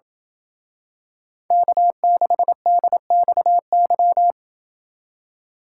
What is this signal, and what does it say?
Morse code, heard as a keyed tone, about 27 wpm, 700 Hz. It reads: K6DXY